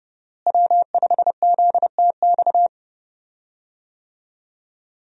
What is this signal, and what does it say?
Morse code, heard as a keyed tone, about 30 wpm, 700 Hz. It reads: W5ZTX
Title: USA Call Signs - 5 Characters - Repeated 2x - 30wpm